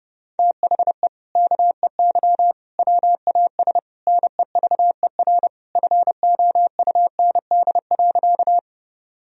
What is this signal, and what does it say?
Morse code, heard as a keyed tone, about 30 wpm, 700 Hz. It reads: THE KEY WAS NEVER FOUND.
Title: Sentences from Top 1000 Words - Review Part 4 - 30wpm